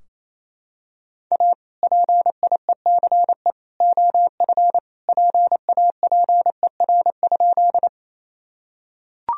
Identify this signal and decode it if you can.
Morse code, heard as a keyed tone, about 29 wpm, 700 Hz. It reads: A PIECE OF PAPER?
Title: Sentences from Top 1000 Words - Review Part 2 - 28wpm